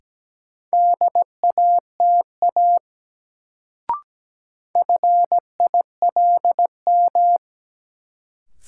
Morse code, heard as a keyed tone, about 17 wpm, 700 Hz. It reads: DATA FILM
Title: Sentences from Top 1000 Words - Review Part 3 - New Word Drill - 17wpm